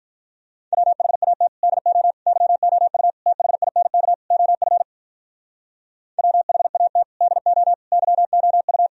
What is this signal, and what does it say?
Morse code, heard as a keyed tone, about 46 wpm, 700 Hz. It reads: WHAT DO YOU THINK OF WHAT DO YOU